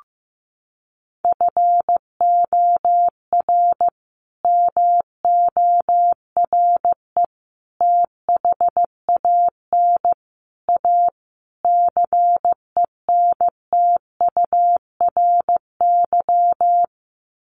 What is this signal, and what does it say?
Morse code, heard as a keyed tone, about 15 wpm, 700 Hz. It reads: FOR MORE THAN A CENTURY
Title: Sets of 5 Words Encore2 15wpm